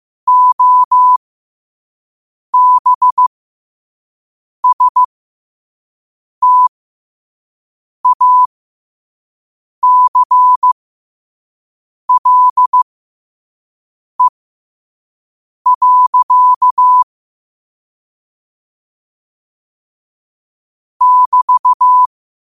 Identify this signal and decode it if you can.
Morse code, heard as a keyed tone, about 15 wpm, spaced as for 5 wpm, 1000 Hz. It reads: OBSTACLE. <BT>
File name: qotdfiles / 5wpm